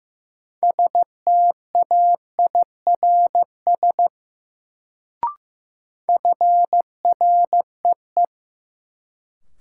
Morse code, heard as a keyed tone, about 15 wpm, 700 Hz. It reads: STAIRS FREE
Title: Sets of 5 Words Encore - New Word Drill - 15wpm